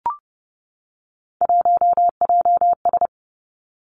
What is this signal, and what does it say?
Morse code, heard as a keyed tone, about 30 wpm, 700 Hz. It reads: 1JS
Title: Three Challenging Letter-Number Combinations 30wpm